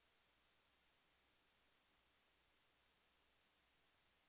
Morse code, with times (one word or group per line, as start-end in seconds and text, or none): none